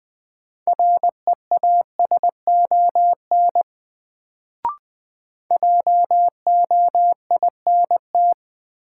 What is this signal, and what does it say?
Morse code, heard as a keyed tone, about 20 wpm, 700 Hz. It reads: REASON JOINT
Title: Sets of 5 Words Encore - New Word Drill - 20wpm